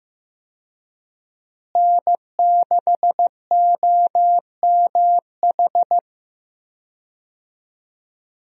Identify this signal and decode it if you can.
Morse code, heard as a keyed tone, about 15 wpm, 700 Hz. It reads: N6OMH
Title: USA Call Signs - 5 Characters - Repeated 3x - 15wpm